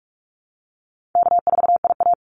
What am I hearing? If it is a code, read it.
Morse code, heard as a keyed tone, about 44 wpm, 700 Hz. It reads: K4IA
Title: Easy Call Signs Repeated 3x - 45wpm